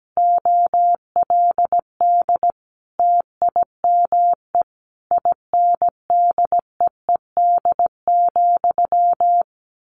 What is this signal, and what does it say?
Morse code, heard as a keyed tone, about 17 wpm, 700 Hz. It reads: OLD TIME INDEED,
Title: A Sound Of Thunder 17wpm